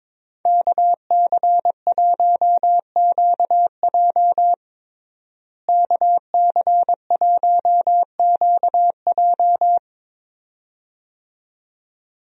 Morse code, heard as a keyed tone, about 22 wpm, 700 Hz. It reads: KC1QJ KC1QJ
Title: USA Call Signs - 5 Characters - Repeated 2x - 22wpm